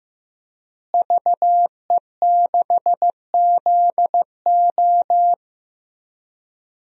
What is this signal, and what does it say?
Morse code, heard as a keyed tone, about 15 wpm, 700 Hz. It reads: VE6ZO
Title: DXCC-100 Call Signs - 5 Characters - Repeated 2x - 15wpm